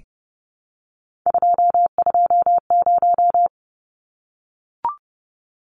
Morse code, heard as a keyed tone, about 30 wpm, 700 Hz. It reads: 220
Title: Numbers - 3-Digit 30wpm